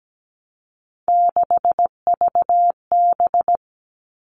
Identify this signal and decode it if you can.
Morse code, heard as a keyed tone, about 17 wpm, 700 Hz. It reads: 6VB
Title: All Cut Numbers - 3-Digit 17wpm